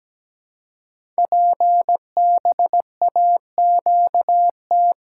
Morse code, heard as a keyed tone, about 17 wpm, 700 Hz. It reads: PBAQT
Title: Code Groups - 5 Letters - 17wpm